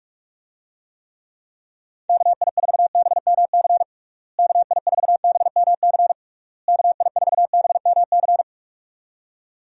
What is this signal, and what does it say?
Morse code, heard as a keyed tone, about 46 wpm, 700 Hz. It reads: KI4BMC KI4BMC KI4BMC
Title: USA Call Signs - 6 Characters - Repeated 3x - 45wpm